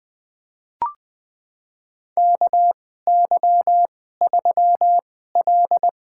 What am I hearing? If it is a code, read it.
Morse code, heard as a keyed tone, about 20 wpm, spaced as for 15 wpm, 700 Hz. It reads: KY3L
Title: USA Call Signs - 4 Characters - 2x Character Spacing - 20wpm